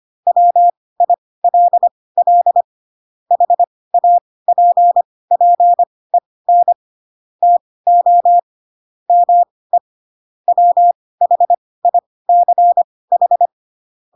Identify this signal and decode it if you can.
Morse code, heard as a keyed tone, about 25 wpm, spaced as for 18 wpm, 700 Hz. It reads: WILL HAPPEN TO ME WHICH
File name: Pinocchio / Pinocchio ch4 12 25 ik1hge